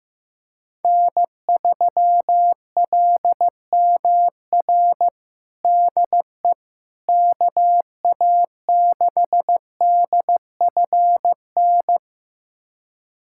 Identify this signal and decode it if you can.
Morse code, heard as a keyed tone, about 15 wpm, 700 Hz. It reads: N3LMR DE KA6DFN
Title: USA Callsign DE Callsign 15wpm